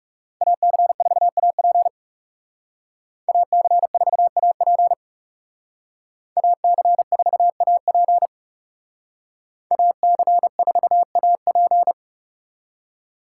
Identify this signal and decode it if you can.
Morse code, heard as a keyed tone, about 38 wpm, 700 Hz. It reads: AC4AP AC4AP AC4AP AC4AP